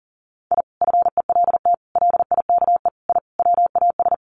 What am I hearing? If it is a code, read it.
Morse code, heard as a keyed tone, about 40 wpm, 700 Hz. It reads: I FELT LIKE I WAS
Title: Sets of 5 Words Encore 40wpm